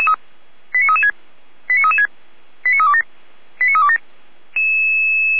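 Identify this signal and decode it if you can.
Five-tone selective call (ZVEI), heard as quick runs of five tones, tones of about 70 ms. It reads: 89336 89336 89327 89327
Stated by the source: ZVEI five-tone selective call received on an SDR.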